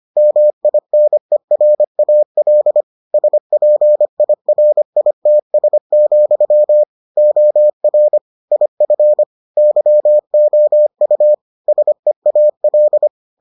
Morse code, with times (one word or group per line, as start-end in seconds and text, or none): 0.2-2.8 MINERAL
3.1-6.8 SPIRITS,
7.2-8.2 OR
8.5-9.2 IF
9.6-11.3 YOU
11.7-13.1 SEAL